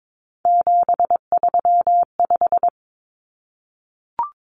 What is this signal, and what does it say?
Morse code, heard as a keyed tone, about 22 wpm, 700 Hz. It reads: 735
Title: Common Cut Numbers - 3-Digit 22wpm